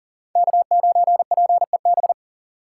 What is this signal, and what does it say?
Morse code, heard as a keyed tone, about 37 wpm, 700 Hz. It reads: K9PEB
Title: USA Call Signs - 5 Characters - 6x Character Spacing - 40wpm